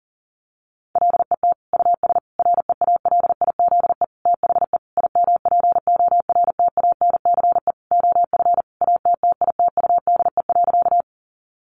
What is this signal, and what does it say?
Morse code, heard as a keyed tone, about 40 wpm, 700 Hz. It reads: LET US REALIZE THE IMPORTANCE OF ATTITUDE.
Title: Sentences from Top 1300 Words 40wpm